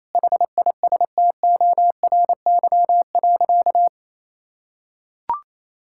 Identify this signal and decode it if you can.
Morse code, heard as a keyed tone, about 28 wpm, 700 Hz. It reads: HISTORY.
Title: Sentences from Top 1000 Words - Review Part 5 - 28wpm